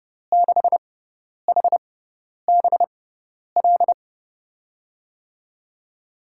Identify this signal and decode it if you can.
Morse code, heard as a keyed tone, about 30 wpm, spaced as for 10 wpm, 700 Hz. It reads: 6HBL
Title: USA Call Signs - 6x Character Spacing - 30wpm